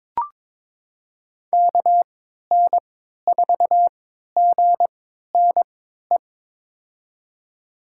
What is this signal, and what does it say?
Morse code, heard as a keyed tone, about 22 wpm, spaced as for 13 wpm, 700 Hz. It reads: KN4GNE
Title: USA Call Signs - 6 Characters - 3x Character Spacing - 22wpm